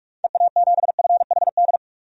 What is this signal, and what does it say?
Morse code, heard as a keyed tone, about 45 wpm, 700 Hz. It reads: EA7FHD
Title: DXCC-100 Call Signs - 6 Characters - 45wpm